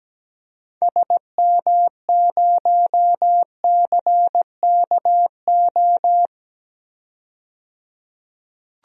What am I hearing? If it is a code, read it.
Morse code, heard as a keyed tone, about 17 wpm, 700 Hz. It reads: SM0CKO
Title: DXCC-100 Call Signs - 6 Characters - 17wpm